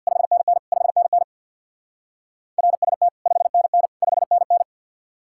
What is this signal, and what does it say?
Morse code, heard as a keyed tone, about 65 wpm, 700 Hz. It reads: 5NN 5NN RST 5NN 5NN